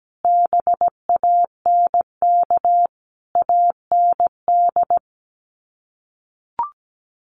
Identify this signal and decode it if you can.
Morse code, heard as a keyed tone, about 17 wpm, 700 Hz. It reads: BANK AND
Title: Sets of 5 Words Encore2 17wpm